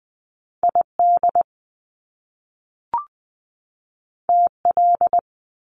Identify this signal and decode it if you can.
Morse code, heard as a keyed tone, about 20 wpm, 700 Hz. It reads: ID TL
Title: Top Two Letter Combinations in Words 20wpm